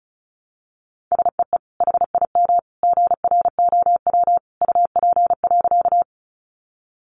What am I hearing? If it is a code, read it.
Morse code, heard as a keyed tone, about 34 wpm, 700 Hz. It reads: SEE HIM GROW UP.
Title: Sentences from Top 300 Words 35wpm